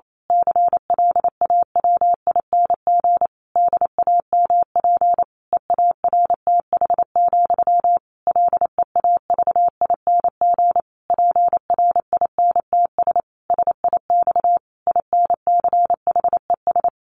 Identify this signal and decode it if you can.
Morse code, heard as a keyed tone, about 28 wpm, 700 Hz. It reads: CLAWING DAMP EARTH, LEAVING PRINTS SIX INCHES
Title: A Sound Of Thunder 28wpm